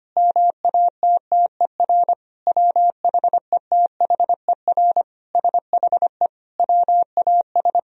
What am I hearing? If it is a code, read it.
Morse code, heard as a keyed tone, about 25 wpm, 700 Hz. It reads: MATTER WHETHER SHE WAS
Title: Sentences from Top 1000 Words - Review Part 3 - 25wpm